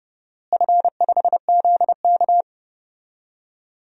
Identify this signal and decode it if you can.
Morse code, heard as a keyed tone, about 30 wpm, 700 Hz. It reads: F5ZK
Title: DXCC-100 Call Signs - 4 Characters - Repeated 2x - 30wpm